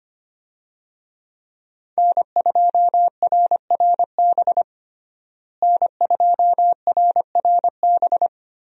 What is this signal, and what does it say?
Morse code, heard as a keyed tone, about 25 wpm, 700 Hz. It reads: N2RRB N2RRB
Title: USA Call Signs - 5 Characters - Repeated 3x - 25wpm